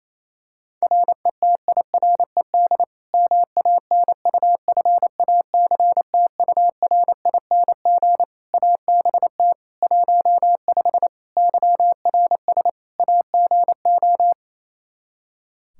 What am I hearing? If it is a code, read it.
Morse code, heard as a keyed tone, about 28 wpm, 700 Hz. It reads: RETIRED MANUFACTURING ABT 15 YRS AGO
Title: Practice QSO - Exchange 3 - More Info - 28wpm